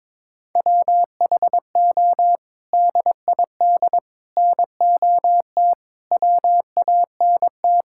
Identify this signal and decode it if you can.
Morse code, heard as a keyed tone, about 22 wpm, 700 Hz. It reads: WHO DID NOT WANT
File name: practice / 22wpm / Sets of 8 Words - 22wpm